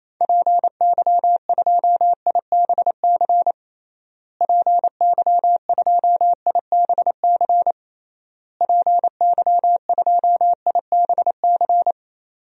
Morse code, heard as a keyed tone, about 28 wpm, 700 Hz. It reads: PY2IBC PY2IBC PY2IBC
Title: DXCC-100 Call Signs - 6 Characters - Repeated 3x - 28wpm